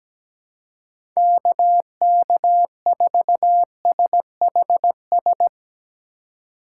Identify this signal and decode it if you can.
Morse code, heard as a keyed tone, about 17 wpm, 700 Hz. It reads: KK4SHS
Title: USA Call Signs - 6 Characters - 3x Character Spacing - 17wpm